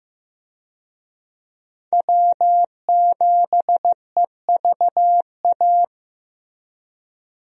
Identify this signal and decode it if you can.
Morse code, heard as a keyed tone, about 15 wpm, 700 Hz. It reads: W7EVA
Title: USA Call Signs - 5 Characters - Repeated 3x - 15wpm